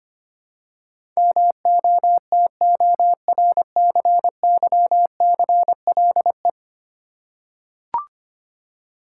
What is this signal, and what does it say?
Morse code, heard as a keyed tone, about 25 wpm, 700 Hz. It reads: MOTORCYCLE
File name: Ten Letter Words - 25wpm